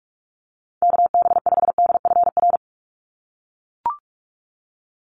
Morse code, heard as a keyed tone, about 44 wpm, 700 Hz. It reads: KB5DFR
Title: USA Call Signs 45wpm